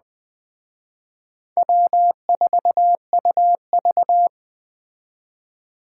Morse code, heard as a keyed tone, about 19 wpm, 700 Hz. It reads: W4UV